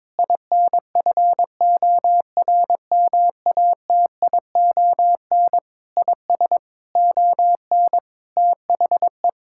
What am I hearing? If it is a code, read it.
Morse code, heard as a keyed tone, about 22 wpm, 700 Hz. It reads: INFORMATION IS ON THE
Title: Sentences from Top 1000 Words - Review Part 6 - 22wpm